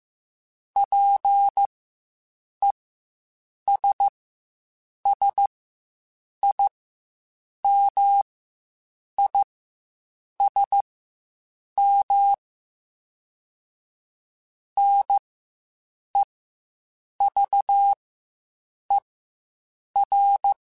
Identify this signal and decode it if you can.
Morse code, heard as a keyed tone, about 15 wpm, spaced as for 7 wpm, 800 Hz. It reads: PESSIMISM NEVER